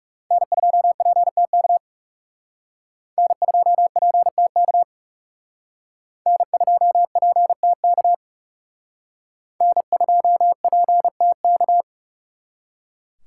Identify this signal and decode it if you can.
Morse code, heard as a keyed tone, about 37 wpm, 700 Hz. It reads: N2PTK N2PTK N2PTK N2PTK
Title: Speed Racing - USA 1x3 Call Signs 30wpm